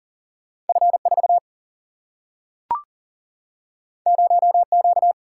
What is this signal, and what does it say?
Morse code, heard as a keyed tone, about 40 wpm, 700 Hz. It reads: F4 0Q
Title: Single-Letter Single-Number 40wpm